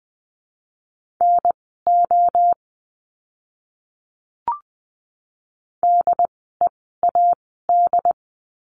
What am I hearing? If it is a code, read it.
Morse code, heard as a keyed tone, about 20 wpm, spaced as for 14 wpm, 700 Hz. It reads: NO DEAD
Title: 075 Lesson 13f - Letter L - Words 20wpm